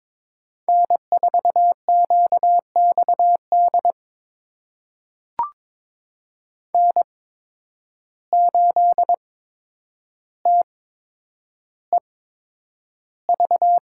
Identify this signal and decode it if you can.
Morse code, heard as a keyed tone, about 21 wpm, spaced as for 9 wpm, 700 Hz. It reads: N4QXD N8TEV
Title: USA Call Signs - 5 Characters - 8x Character Spacing - 22wpm